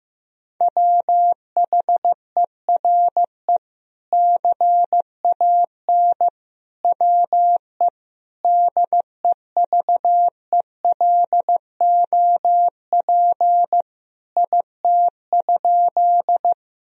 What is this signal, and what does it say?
Morse code, heard as a keyed tone, about 15 wpm, 700 Hz. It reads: WHERE CAN WE DEVELOP IT?
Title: Sentences from Top 1000 Words - Review Part 2 - 15wpm